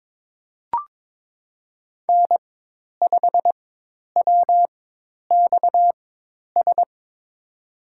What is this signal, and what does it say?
Morse code, heard as a keyed tone, about 22 wpm, spaced as for 10 wpm, 700 Hz. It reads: N5WXS
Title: USA Call Signs - 5 Characters - 4x Character Spacing - 22wpm